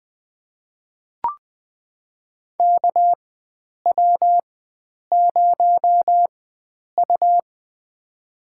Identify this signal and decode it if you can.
Morse code, heard as a keyed tone, about 20 wpm, spaced as for 9 wpm, 700 Hz. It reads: KW0U
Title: USA Call Signs - 4 Characters - 4x Character Spacing - 20wpm